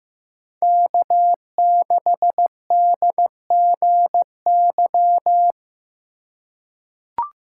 Morse code, heard as a keyed tone, about 15 wpm, 700 Hz. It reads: K6DGY
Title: USA Call Signs - 5 Characters - 4x Character Spacing - 15wpm